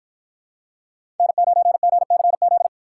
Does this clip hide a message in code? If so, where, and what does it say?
Yes: Morse code from 1.2 to 2.7 s, heard as a keyed tone, about 53 wpm, 700 Hz. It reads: N9GCZ